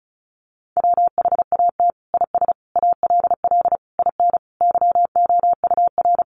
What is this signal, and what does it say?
Morse code, heard as a keyed tone, about 34 wpm, 700 Hz. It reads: WHAT IS ALL IN YOUR
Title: Sentences from Top 1200 Words 35wpm